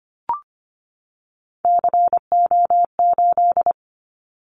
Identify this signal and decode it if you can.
Morse code, heard as a keyed tone, about 24 wpm, 700 Hz. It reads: CO8
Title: Callsign Prefixes for Top 100 DXCC - 25wpm